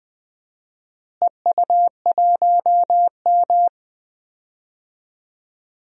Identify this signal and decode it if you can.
Morse code, heard as a keyed tone, about 21 wpm, 700 Hz. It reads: EU1M
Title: DXCC-100 Call Signs - 4 Characters - Repeated 2x - 20wpm